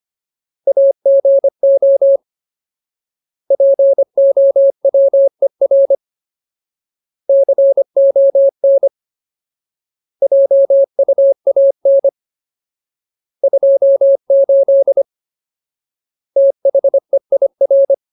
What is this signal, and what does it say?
Morse code, heard as a keyed tone, about 25 wpm, 550 Hz. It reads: AGO POWER CON JUAN 28 THEIR
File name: Ultimate Word List / 25WPM Ultimate